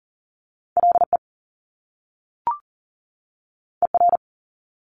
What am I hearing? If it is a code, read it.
Morse code, heard as a keyed tone, about 39 wpm, 700 Hz. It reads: LE ER
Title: Top Two Letter Combinations in Words 40wpm